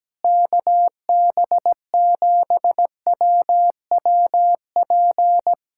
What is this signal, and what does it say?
Morse code, heard as a keyed tone, about 17 wpm, 700 Hz. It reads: KB7WWP
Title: Speed-Racing Common QSO Phrases - 17wpm